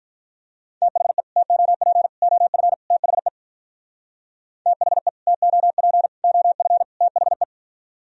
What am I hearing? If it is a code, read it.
Morse code, heard as a keyed tone, about 43 wpm, 700 Hz. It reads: THE TOP OF THE THE TOP OF THE
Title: Speed-Racing - Sets of 4 Words 35wpm